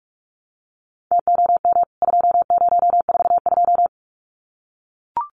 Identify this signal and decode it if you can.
Morse code, heard as a keyed tone, about 44 wpm, 700 Hz. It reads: TOM 2042
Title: CWT Exchange 45wpm